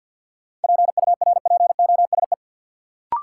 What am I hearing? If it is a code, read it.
Morse code, heard as a keyed tone, about 53 wpm, 700 Hz. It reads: PURPOSE
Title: Sentences from Top 800 Words - New Word Drill - 50wpm